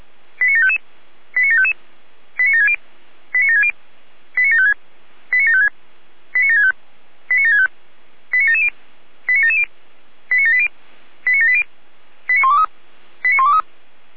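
Five-tone selective call (ZVEI), heard as quick runs of five tones, tones of about 80 ms. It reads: 89755 89755 89760 89760 89756 89756 89765 89765 89770 89770 89780 89780 89123 89123
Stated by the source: ZVEI five-tone call received by SDR.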